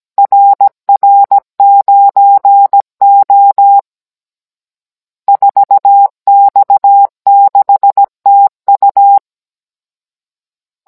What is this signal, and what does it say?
Morse code, heard as a keyed tone, about 17 wpm, 800 Hz. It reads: RR9O 4X6TU
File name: Cours-NR19-16mm